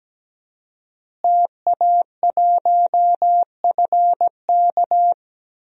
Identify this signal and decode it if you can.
Morse code, heard as a keyed tone, about 17 wpm, 700 Hz. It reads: TA1FK
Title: DXCC-100 Call Signs - 5 Characters - Repeated 3x - 17wpm